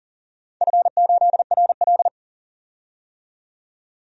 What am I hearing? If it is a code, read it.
Morse code, heard as a keyed tone, about 39 wpm, 700 Hz. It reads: F8RL
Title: DXCC-100 Call Signs - 4 Characters - 40wpm